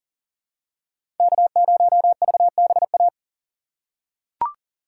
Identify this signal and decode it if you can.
Morse code, heard as a keyed tone, about 40 wpm, 700 Hz. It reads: K0VBA
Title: USA Call Signs - 4x Character Spacing - 40wpm